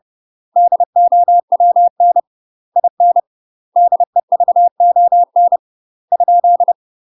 Morse code, heard as a keyed tone, about 30 wpm, 700 Hz. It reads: DOWN IN DEVON ?
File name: UK MASTER QSOs TRAINING / 30WPM / UK MASTER QSOs 0020